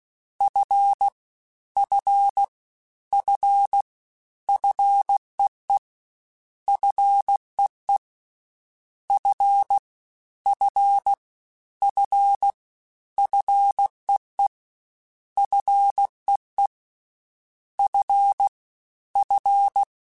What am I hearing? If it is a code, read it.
Morse code, heard as a keyed tone, about 16 wpm, 800 Hz. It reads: F F F FEE FEE F F F FEE FEE F F
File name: Single F mp3s / Single F Prefixes 3x2 Twice @15